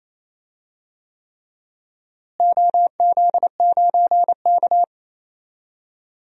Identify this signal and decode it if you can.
Morse code, heard as a keyed tone, about 28 wpm, 700 Hz. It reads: OZ9K
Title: DXCC-100 Call Signs - 4 Characters - Repeated 3x - 28wpm